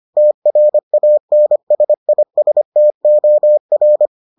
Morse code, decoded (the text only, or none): TRANSISTOR